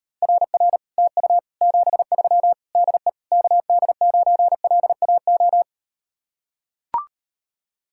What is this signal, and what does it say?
Morse code, heard as a keyed tone, about 38 wpm, 700 Hz. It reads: RR TU 73 DE KD9LAO